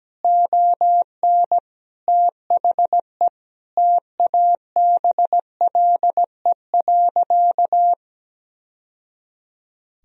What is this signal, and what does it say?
Morse code, heard as a keyed tone, about 17 wpm, 700 Hz. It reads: ON THE TABLE.
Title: Sentences from Top 1000 Words - Review Part 2 - 17wpm